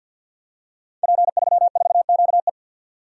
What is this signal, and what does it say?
Morse code, heard as a keyed tone, about 52 wpm, 700 Hz. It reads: P34QE
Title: Code Groups - 5 Letter:Numbers - 50wpm